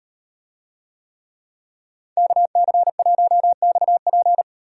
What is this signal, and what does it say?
Morse code, heard as a keyed tone, about 35 wpm, 700 Hz. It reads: KC1XP